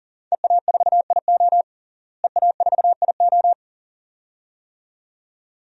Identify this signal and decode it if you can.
Morse code, heard as a keyed tone, about 37 wpm, 700 Hz. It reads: EA4IO EA4IO
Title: DXCC-100 Call Signs - 5 Characters - Repeated 2x - 40wpm